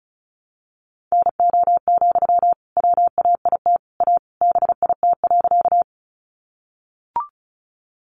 Morse code, heard as a keyed tone, about 34 wpm, 700 Hz. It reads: NO, WAIT A BIT.
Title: Sentences from Top 900 Words 35wpm